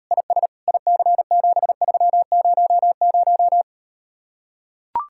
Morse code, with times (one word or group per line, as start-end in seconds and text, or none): 0.1-0.5 IS
0.7-3.6 IC7300